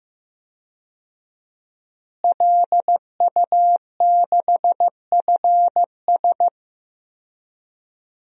Morse code, heard as a keyed tone, about 15 wpm, 700 Hz. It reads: LU6FS
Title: DXCC-100 Call Signs - 5 Characters - Repeated 2x - 15wpm